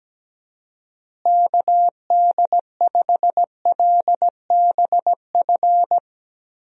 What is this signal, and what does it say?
Morse code, heard as a keyed tone, about 17 wpm, 700 Hz. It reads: KD5LBF
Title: USA Call Signs - 6 Characters - Repeated 3x - 17wpm